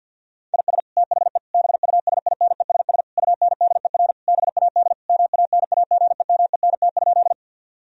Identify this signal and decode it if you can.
Morse code, heard as a keyed tone, about 50 wpm, 700 Hz. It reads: IS THE BUSINESS UNDER BAD MANAGEMENT?
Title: Sentences from Top 1000 Words - Review Part 4 - 50wpm